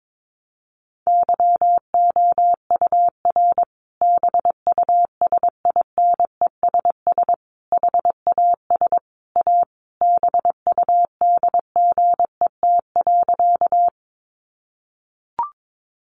Morse code, heard as a keyed tone, about 22 wpm, 700 Hz. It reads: YOUR BUSINESS HAS A BUDGET.
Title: Sentences from Top 1200 Words 22wpm